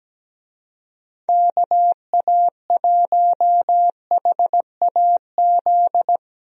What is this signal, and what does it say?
Morse code, heard as a keyed tone, about 17 wpm, 700 Hz. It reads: KA1HAZ
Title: USA Call Signs - 6 Characters - Repeated 2x - 17wpm